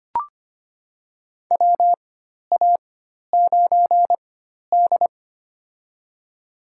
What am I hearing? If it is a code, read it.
Morse code, heard as a keyed tone, about 25 wpm, spaced as for 12 wpm, 700 Hz. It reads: WA9D
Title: USA Call Signs - 4 Characters - 4x Character Spacing - 25wpm